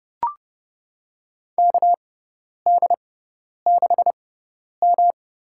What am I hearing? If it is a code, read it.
Morse code, heard as a keyed tone, about 30 wpm, spaced as for 10 wpm, 700 Hz. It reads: KD6M
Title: USA Call Signs - 4 Characters - 6x Character Spacing - 30wpm